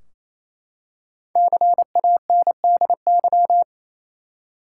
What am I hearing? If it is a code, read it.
Morse code, heard as a keyed tone, about 28 wpm, 700 Hz. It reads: CANDY
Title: Five Letter Words - 28wpm